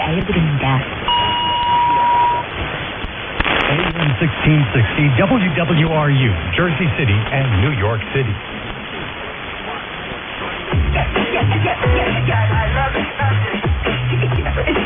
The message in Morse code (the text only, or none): none